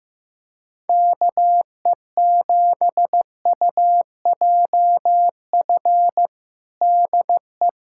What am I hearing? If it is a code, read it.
Morse code, heard as a keyed tone, about 15 wpm, 700 Hz. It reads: KE7UJF DE
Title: USA Callsign DE Callsign - 6 Characters - 15wpm